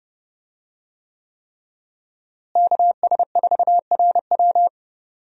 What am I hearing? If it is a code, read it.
Morse code, heard as a keyed tone, about 30 wpm, 700 Hz. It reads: KS4RW
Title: USA Call Signs - 5 Characters - Repeated 3x - 30wpm